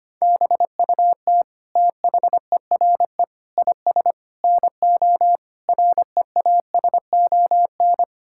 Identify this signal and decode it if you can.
Morse code, heard as a keyed tone, about 25 wpm, 700 Hz. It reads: BUT THERE IS NO REASON
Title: Sentences from Top 1000 Words - Review Part 6 - 25wpm